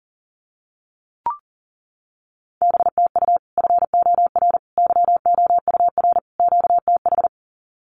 Morse code, heard as a keyed tone, about 40 wpm, 700 Hz. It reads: BTU FOR YOUR QTH
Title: Common QSO Phrases 40wpm